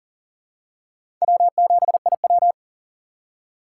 Morse code, heard as a keyed tone, about 40 wpm, 700 Hz. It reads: W7IW
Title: USA Call Signs - 3x Character Spacing - 40wpm